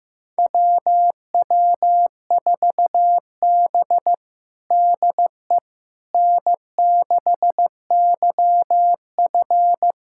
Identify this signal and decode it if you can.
Morse code, heard as a keyed tone, about 15 wpm, 700 Hz. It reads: WW4B DE N6YF
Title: USA Callsign DE Callsign - 4 Characters - 15wpm